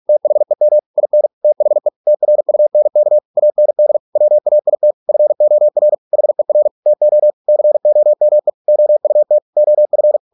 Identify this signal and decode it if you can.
Morse code, heard as a keyed tone, about 46 wpm, 600 Hz. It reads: THEM IN THE TRUNK AND WAIT FOR HER TO COME OUT OF